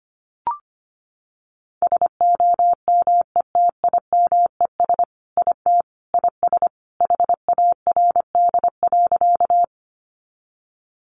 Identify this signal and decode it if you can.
Morse code, heard as a keyed tone, about 25 wpm, 700 Hz. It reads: SOMETIMES IT IS HARD.
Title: Sentences from Top 600 Words 25wpm